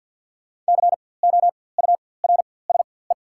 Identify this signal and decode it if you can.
Morse code, heard as a keyed tone, about 53 wpm, spaced as for 24 wpm, 690 Hz. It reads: COURSE
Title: 4x Character Spacing - Top 500 Words - 50wpm